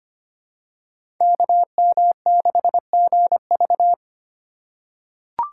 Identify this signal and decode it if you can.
Morse code, heard as a keyed tone, about 25 wpm, 700 Hz. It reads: KM6GV
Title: USA Call Signs - 2x Character Spacing - 25wpm